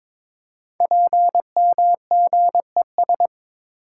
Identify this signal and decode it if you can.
Morse code, heard as a keyed tone, about 22 wpm, 700 Hz. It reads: PMGES
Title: Code Groups - 5 Letters - 22wpm